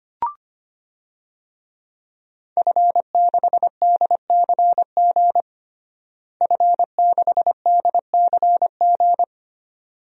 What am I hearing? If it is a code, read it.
Morse code, heard as a keyed tone, about 25 wpm, 700 Hz. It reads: F6DCG F6DCG
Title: DXCC-100 Call Signs - 5 Characters - Repeated 2x - 25wpm